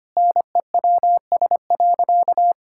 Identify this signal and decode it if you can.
Morse code, heard as a keyed tone, about 25 wpm, 700 Hz. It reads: NEWS.
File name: practice / 25wpm / Sentences with 3-Word Phrases - 25wpm